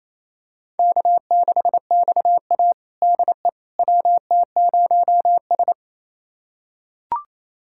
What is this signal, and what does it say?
Morse code, heard as a keyed tone, about 28 wpm, 700 Hz. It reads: K6XA DE WT0S
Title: USA Callsign DE Callsign - 4 Characters - 28wpm